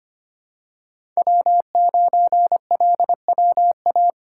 Morse code, heard as a keyed tone, about 25 wpm, 700 Hz. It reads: W9LWA
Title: USA Call Signs - 5 Characters - 8x Character Spacing - 25wpm